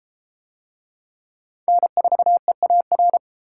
Morse code, heard as a keyed tone, about 31 wpm, 700 Hz. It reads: N4EAR